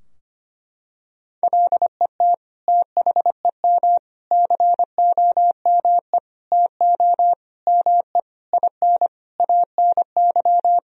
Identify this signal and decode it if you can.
Morse code, heard as a keyed tone, about 25 wpm, 700 Hz. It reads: LET THEM COME TO ME IN ANY
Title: Sentences from Top 1000 Words - Review Part 3 - 25wpm